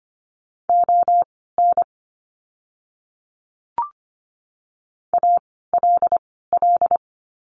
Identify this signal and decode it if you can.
Morse code, heard as a keyed tone, about 25 wpm, spaced as for 16 wpm, 700 Hz. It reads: ON ALL
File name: course / 25:10wpm / 094 Lesson 16f - Letter c - Words 25wpm